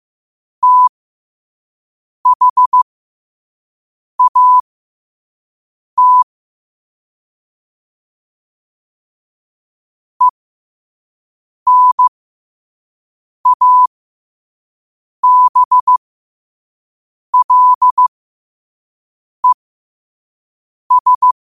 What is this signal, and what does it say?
Morse code, heard as a keyed tone, about 15 wpm, spaced as for 5 wpm, 1000 Hz. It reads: THAT ENABLES
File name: 2024-10-07-5wpm